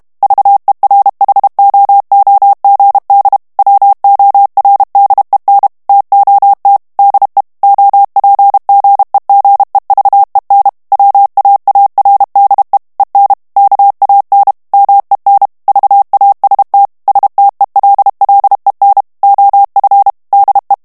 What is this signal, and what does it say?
Morse code, heard as a keyed tone, about 32 wpm, 800 Hz. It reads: VERHOOGD WORDEN TOT DE OPGEGEVEN WAARDE EN KAN MEN VAST STELLEN OF DE
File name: HOLLANDAIS texte clair 2 vitesse 30 mots minute